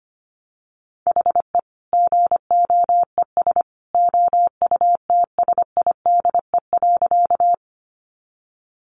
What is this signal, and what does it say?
Morse code, heard as a keyed tone, about 25 wpm, 700 Hz. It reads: HE GOES OUTSIDE.
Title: Additional Easy Phrases 25wpm